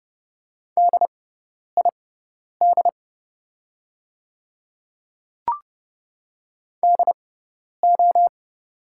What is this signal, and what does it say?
Morse code, heard as a keyed tone, about 30 wpm, spaced as for 10 wpm, 700 Hz. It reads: DID DO
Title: 6x Character Spacing - Top 100 Words - 30wpm